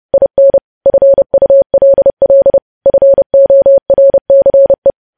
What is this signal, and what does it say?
Morse code, heard as a keyed tone, about 30 wpm, 550 Hz. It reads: IN FULL FORCE
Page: news headlines in morse code at 30 wpm